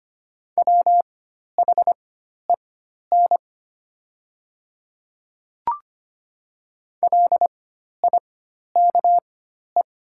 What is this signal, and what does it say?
Morse code, heard as a keyed tone, about 25 wpm, spaced as for 12 wpm, 700 Hz. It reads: WHEN LIKE
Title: 4x Character Spacing - Top 100 Words - 25wpm